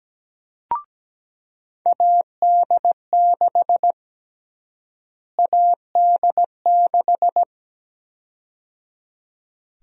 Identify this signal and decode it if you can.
Morse code, heard as a keyed tone, about 17 wpm, 700 Hz. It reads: AD6 AD6
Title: USA Call Sign Prefixes - Repeated 2x - 17wpm